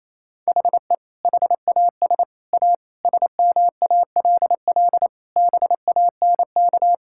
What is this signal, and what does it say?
Morse code, heard as a keyed tone, about 28 wpm, 700 Hz. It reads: HE HAS A SMALL BANK
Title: Sentences from Top 1000 Words - Review Part 6 - 28wpm